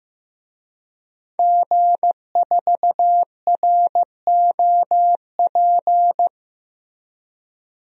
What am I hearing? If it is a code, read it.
Morse code, heard as a keyed tone, about 15 wpm, 700 Hz. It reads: G4ROP
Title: DXCC-100 Call Signs - 5 Characters - Repeated 2x - 15wpm